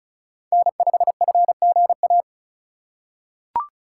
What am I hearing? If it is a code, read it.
Morse code, heard as a keyed tone, about 36 wpm, 700 Hz. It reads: N5FGA
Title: USA Call Signs - 5 Characters - 2x Character Spacing - 35wpm